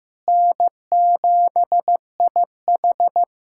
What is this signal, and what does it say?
Morse code, heard as a keyed tone, about 16 wpm, 700 Hz. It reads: N7IH
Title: USA Callsign DE Callsign - 4 Characters - 15wpm